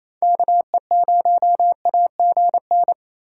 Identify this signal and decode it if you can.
Morse code, heard as a keyed tone, about 27 wpm, 700 Hz. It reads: KE0AGN
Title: USA Call Signs - 2x Character Spacing - 28wpm